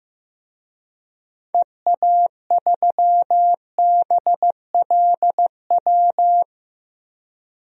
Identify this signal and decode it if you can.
Morse code, heard as a keyed tone, about 16 wpm, 700 Hz. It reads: EA3BLW
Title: DXCC-100 Call Signs - 6 Characters - Repeated 3x - 15wpm